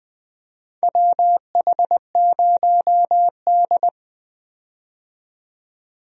Morse code, heard as a keyed tone, about 20 wpm, 700 Hz. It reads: WH0D
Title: USA Call Signs - 4 Characters - 20wpm